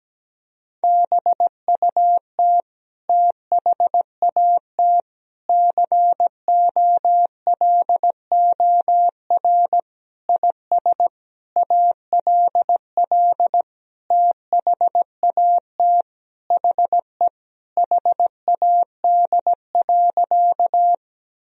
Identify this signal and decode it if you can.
Morse code, heard as a keyed tone, about 17 wpm, 700 Hz. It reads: BUT THAT COLOR IS ALL THAT HE HAD.
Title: Sentences from Top 1000 Words - Review Part 2 - 17wpm